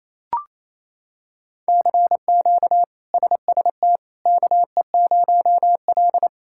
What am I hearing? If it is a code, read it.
Morse code, heard as a keyed tone, about 28 wpm, 700 Hz. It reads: CQ SST KE0L
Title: K1USN SST QSO Practice - 28wpm